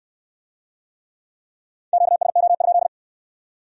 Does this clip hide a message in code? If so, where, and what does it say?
Yes: Morse code from 1.9 to 2.9 s, heard as a keyed tone, about 63 wpm, 700 Hz. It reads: QSO?